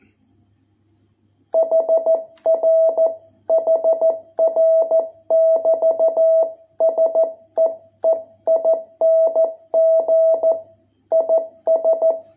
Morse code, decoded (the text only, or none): HR HR <BT> SEEING IS